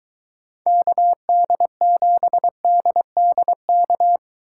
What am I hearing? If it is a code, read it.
Morse code, heard as a keyed tone, about 24 wpm, 700 Hz. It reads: KD7DDK